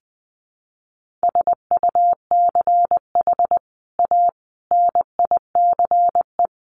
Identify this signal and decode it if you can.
Morse code, heard as a keyed tone, about 20 wpm, 700 Hz. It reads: SUCH A NICE
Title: Sentences from Top 1100 Words 20wpm